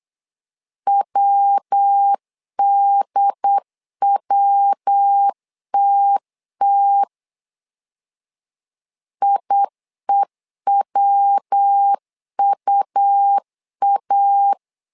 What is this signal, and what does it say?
Morse code, heard as a keyed tone, about 8 wpm, 800 Hz. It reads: WDWTT IEWUA
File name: LEZIONI / 3 MWRO / 3 MW 8wpm 2501